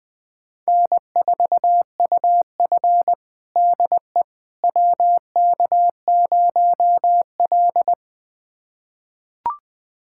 Morse code, heard as a keyed tone, about 20 wpm, 700 Hz. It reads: N4UF DE WK0L
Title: USA Callsign DE Callsign - 4 Characters - 20wpm